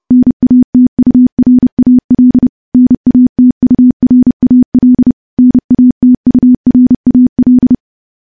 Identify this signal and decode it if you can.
Morse code, heard as a keyed tone, about 30 wpm, 250 Hz. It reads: NATURAL NATURAL NATURAL
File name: morse / morse-training-30WPM